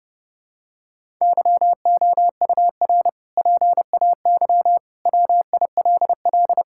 Morse code, heard as a keyed tone, about 32 wpm, 700 Hz. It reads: YOUR PAY WILL
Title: Sentences from Top 1000 Words - Review Part 6 - 30wpm